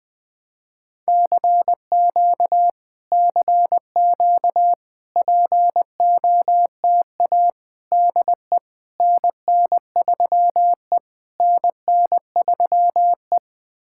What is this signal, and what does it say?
Morse code, heard as a keyed tone, about 20 wpm, 700 Hz. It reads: CQ CQ POTA DE NN3E NN3E
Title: POTA QSO Practice - 20wpm